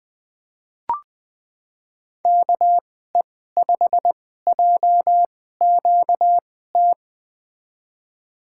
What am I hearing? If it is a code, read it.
Morse code, heard as a keyed tone, about 21 wpm, spaced as for 15 wpm, 700 Hz. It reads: KE5JQT
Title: USA Call Signs - 6 Characters - 2x Character Spacing - 20wpm